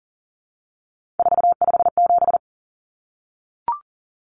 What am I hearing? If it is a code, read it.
Morse code, heard as a keyed tone, about 37 wpm, 700 Hz. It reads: 457
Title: Numbers - 3-Digit 40wpm